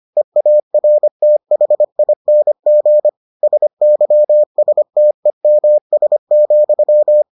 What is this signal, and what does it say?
Morse code, heard as a keyed tone, about 25 wpm, 600 Hz. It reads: EARTHING SYSTEMS,